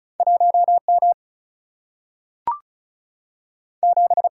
Morse code, heard as a keyed tone, about 36 wpm, 700 Hz. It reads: 1 M 7
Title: Single-Letter Single-Number 35wpm